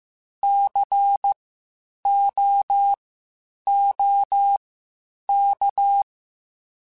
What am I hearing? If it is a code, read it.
Morse code, heard as a keyed tone, about 15 wpm, spaced as for 8 wpm, 800 Hz. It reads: COOK
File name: Spaced Alphas / Spaced Alphas mp3s / Cs sp @13